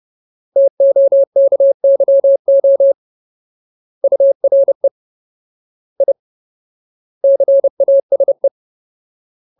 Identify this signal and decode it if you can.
Morse code, heard as a keyed tone, about 30 wpm, 550 Hz. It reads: TOKYO URE I CASE
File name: Ultimate Word List / 30WPM Ultimate